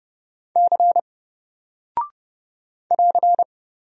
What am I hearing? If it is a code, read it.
Morse code, heard as a keyed tone, about 30 wpm, 700 Hz. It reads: C <AR>
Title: Single Character (Letter:Number:Punctuation:Prosign) Alt 30wpm